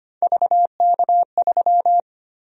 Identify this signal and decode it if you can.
Morse code, heard as a keyed tone, about 25 wpm, 700 Hz. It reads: VK3
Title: USA Call Sign Prefixes - 25wpm